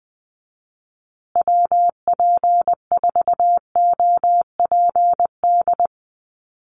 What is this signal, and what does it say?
Morse code, heard as a keyed tone, about 20 wpm, 700 Hz. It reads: WP4OPD
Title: USA Call Signs Repeated 2x 20wpm